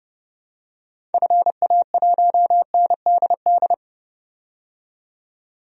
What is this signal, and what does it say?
Morse code, heard as a keyed tone, about 30 wpm, 700 Hz. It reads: FA1NDD
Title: DXCC-100 Call Signs - 6 Characters - 30wpm